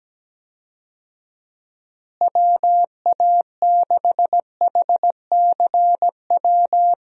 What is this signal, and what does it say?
Morse code, heard as a keyed tone, about 17 wpm, 700 Hz. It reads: WA6HCW
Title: USA Call Signs - 6 Characters - Repeated 2x - 17wpm